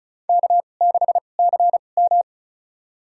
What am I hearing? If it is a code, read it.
Morse code, heard as a keyed tone, about 36 wpm, spaced as for 25 wpm, 690 Hz. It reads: K6CM
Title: USA Call Signs - 4 Characters - 2x Character Spacing - 35wpm